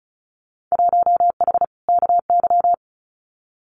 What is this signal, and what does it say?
Morse code, heard as a keyed tone, about 34 wpm, 700 Hz. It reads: 1H KY
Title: Winter Field Day Exchanges 35wpm